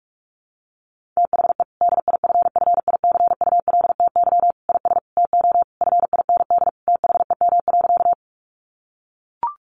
Morse code, heard as a keyed tone, about 45 wpm, 700 Hz. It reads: THE DIFFICULTY IS TO FIND THEM.
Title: Sentences from Top 1500 Words 45wpm